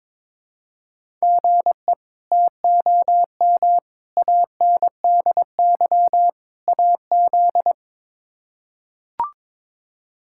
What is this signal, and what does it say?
Morse code, heard as a keyed tone, about 22 wpm, 700 Hz. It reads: GE TOM ANDY AZ
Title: K1USN SST QSO Practice - 22wpm